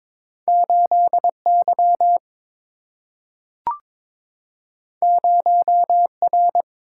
Morse code, heard as a keyed tone, about 22 wpm, 700 Hz. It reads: 8 Y 0 R
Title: Single-Letter Single-Number 22wpm